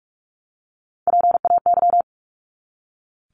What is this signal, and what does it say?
Morse code, heard as a keyed tone, about 44 wpm, 700 Hz. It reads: PAY
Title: Top 501-750 Words 45wpm